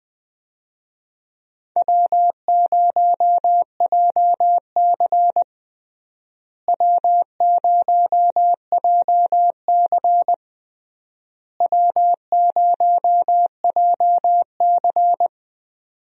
Morse code, heard as a keyed tone, about 20 wpm, 700 Hz. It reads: W0JC W0JC W0JC
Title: USA Call Signs - 4 Characters - Repeated 3x - 20wpm